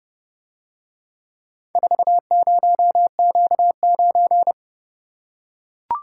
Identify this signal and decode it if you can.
Morse code, heard as a keyed tone, about 29 wpm, 700 Hz. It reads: 40Q9